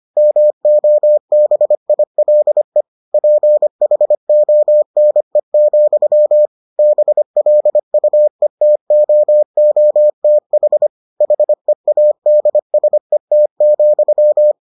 Morse code, heard as a keyed tone, about 25 wpm, 600 Hz. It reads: MOBILE PHONE, BLUETOOTH HEADSET,